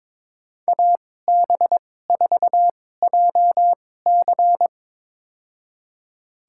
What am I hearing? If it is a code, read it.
Morse code, heard as a keyed tone, about 21 wpm, spaced as for 16 wpm, 700 Hz. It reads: AB4JC
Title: USA Call Signs - 2x Character Spacing - 22wpm